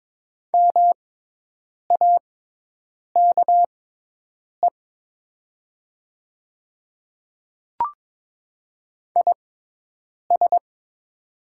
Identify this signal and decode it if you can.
Morse code, heard as a keyed tone, about 22 wpm, spaced as for 8 wpm, 700 Hz. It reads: MAKE IS
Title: 6x Character Spacing - Top 300 Words - 22wpm